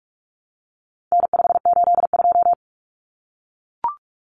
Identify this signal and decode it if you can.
Morse code, heard as a keyed tone, about 44 wpm, 700 Hz. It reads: N582
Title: Common Cut Numbers - 4-Digit 45wpm